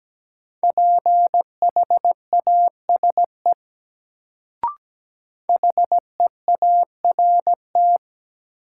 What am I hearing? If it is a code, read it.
Morse code, heard as a keyed tone, about 17 wpm, 700 Hz. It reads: PHASE HEART
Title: Sentences from Top 1200 Words - New Word Drill - 17wpm